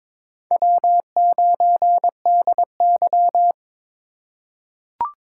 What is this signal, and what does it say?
Morse code, heard as a keyed tone, about 22 wpm, 700 Hz. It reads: W9DY
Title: K1USN SST QSO Practice - 22wpm